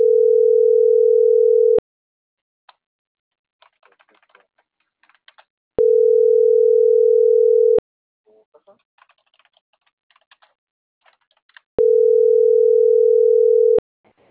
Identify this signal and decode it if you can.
Ringback tone (North America), 440 + 480 Hz.